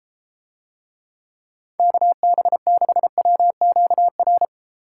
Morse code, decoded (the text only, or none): KB6WQR